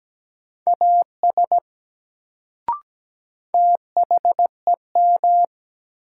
Morse code, heard as a keyed tone, about 17 wpm, 700 Hz. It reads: AS THEM
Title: Sentences from Top 100 Words - New Word Drill - 17wpm